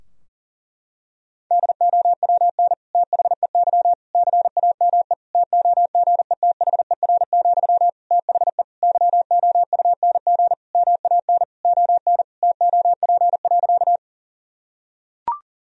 Morse code, heard as a keyed tone, about 40 wpm, 700 Hz. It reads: DOWN THEY CAME TOGETHER, THE YOUNG MAN ON TOP.
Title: Sentences from Top 1000 Words - Review Part 2 - 40wpm